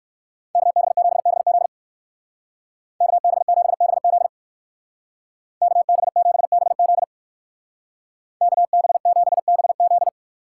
Morse code, heard as a keyed tone, about 59 wpm, 700 Hz. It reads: KB7BZ KB7BZ KB7BZ KB7BZ